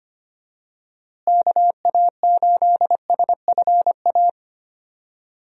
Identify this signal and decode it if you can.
Morse code, heard as a keyed tone, about 25 wpm, 700 Hz. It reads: KA8SFA
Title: USA Call Signs - 6 Characters - Repeated 2x - 25wpm